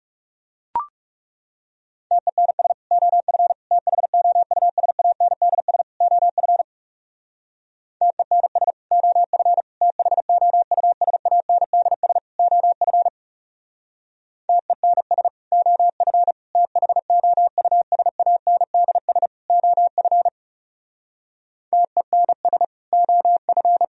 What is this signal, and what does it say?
Morse code, heard as a keyed tone, about 37 wpm, 700 Hz. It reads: TENS OF THOUSANDS OF TENS OF THOUSANDS OF TENS OF THOUSANDS OF TENS OF
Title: Speed-Racing - Sets of 4 Words 30wpm